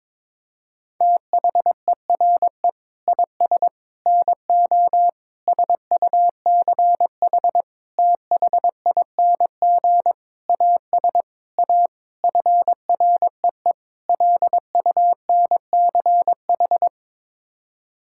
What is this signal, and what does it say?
Morse code, heard as a keyed tone, about 22 wpm, 700 Hz. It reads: THERE IS NO SUCH THING AS A FREE LUNCH
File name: Sets of 9 Words - 22wpm